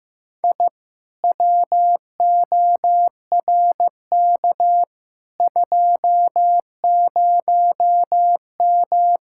Morse code, heard as a keyed tone, about 15 wpm, 700 Hz. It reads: I WORK 20M
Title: Speed-Racing Common QSO Phrases Encore - 15wpm